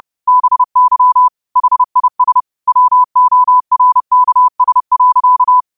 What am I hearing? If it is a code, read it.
Morse code, heard as a keyed tone, about 30 wpm, 1000 Hz. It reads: BY HIS WORKS.
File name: qotdfiles / 30wpm